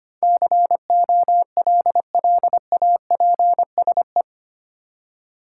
Morse code, heard as a keyed tone, about 25 wpm, 700 Hz. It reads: COLLAPSE
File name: 8 Letter Words - First 3 Letters Given - 25wpm